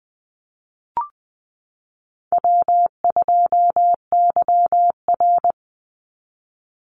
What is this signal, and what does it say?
Morse code, heard as a keyed tone, about 20 wpm, 700 Hz. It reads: W2YR
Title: Easy Call Signs Repeated 2x - 20wpm